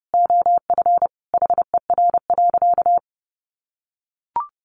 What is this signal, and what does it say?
Morse code, heard as a keyed tone, about 30 wpm, 700 Hz. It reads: OF HER.
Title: Sentences from Top 800 Words 30wpm